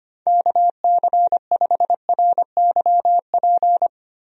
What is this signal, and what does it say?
Morse code, heard as a keyed tone, about 25 wpm, 700 Hz. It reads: KC5RYP
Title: USA Call Signs - 6 Characters - Repeated 2x - 25wpm